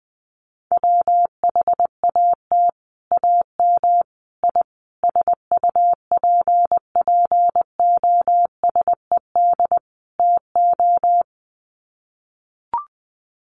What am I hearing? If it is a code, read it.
Morse code, heard as a keyed tone, about 20 wpm, 700 Hz. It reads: WHAT AM I SUPPOSED TO
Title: Sets of 5 Words Encore2 20wpm